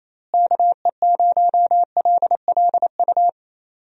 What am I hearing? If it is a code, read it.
Morse code, heard as a keyed tone, about 28 wpm, 700 Hz. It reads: KE0LLU
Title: USA Call Signs - 6 Characters - Repeated 3x - 28wpm